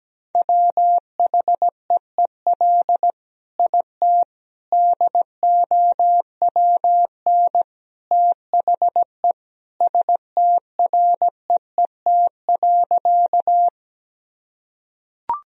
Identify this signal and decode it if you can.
Morse code, heard as a keyed tone, about 17 wpm, 700 Hz. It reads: WHEEL IT DOWN THE STREET.
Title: Sentences from Top 1000 Words - Review Part 3 - 17wpm